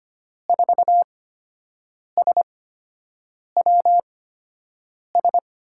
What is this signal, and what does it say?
Morse code, heard as a keyed tone, about 25 wpm, spaced as for 7 wpm, 700 Hz. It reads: 4SWS
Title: USA Call Signs - 8x Character Spacing - 25wpm